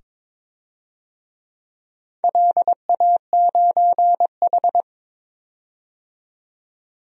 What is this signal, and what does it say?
Morse code, heard as a keyed tone, about 22 wpm, 700 Hz. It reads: LA9H
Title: DXCC-100 Call Signs - 4 Characters - 22wpm